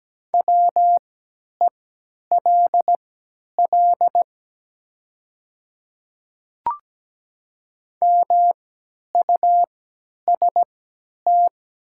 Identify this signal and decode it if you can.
Morse code, heard as a keyed tone, about 17 wpm, spaced as for 10 wpm, 700 Hz. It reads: WELL MUST
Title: 3x Character Spacing - Top 300 Words - 17wpm